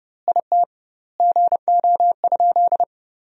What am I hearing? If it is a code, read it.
Morse code, heard as a keyed tone, about 30 wpm, 700 Hz. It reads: IT GO?
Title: 2x Word-Spacing - Sentences from Top 200 Words - 30wpm